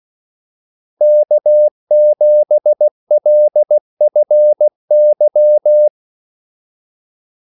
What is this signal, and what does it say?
Morse code, heard as a keyed tone, about 16 wpm, 600 Hz. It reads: K7LFY
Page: Kaldesignaler 16 wpm